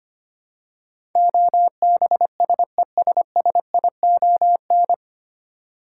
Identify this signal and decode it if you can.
Morse code, heard as a keyed tone, about 25 wpm, 700 Hz. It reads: OBSESSION
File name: Nine Letter Words - 25wpm